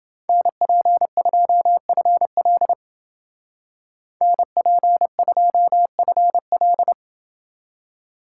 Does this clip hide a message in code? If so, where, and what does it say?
Yes: Morse code from 0.3 to 6.9 s, heard as a keyed tone, about 28 wpm, 700 Hz. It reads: NP2FL NP2FL